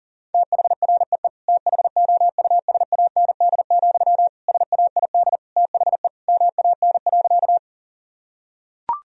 Morse code, heard as a keyed tone, about 37 wpm, 700 Hz. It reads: THREE THOUSAND, SAID THE MAN.
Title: Sentences from Top 1000 Words - Review Part 3 - 40wpm